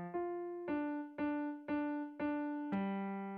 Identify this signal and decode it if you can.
There is no code.